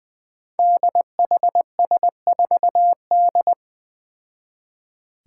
Morse code, heard as a keyed tone, about 20 wpm, 700 Hz. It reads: DHS4D
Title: Five Challenging Letter-Number Combinations 20wpm